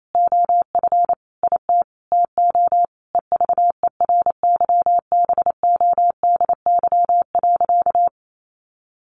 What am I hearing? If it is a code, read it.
Morse code, heard as a keyed tone, about 28 wpm, 700 Hz. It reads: OF IT TO EVERYBODY.
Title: Sentences from Top 1400 Words 28wpm